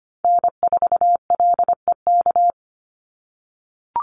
Morse code, heard as a keyed tone, about 25 wpm, 700 Hz. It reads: N4LEK
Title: USA Call Signs 25wpm